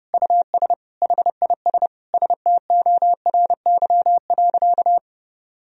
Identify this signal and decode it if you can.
Morse code, heard as a keyed tone, about 30 wpm, 700 Hz. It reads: US HIS STORY.
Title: Sentences from Top 1000 Words - Review Part 1 - 30wpm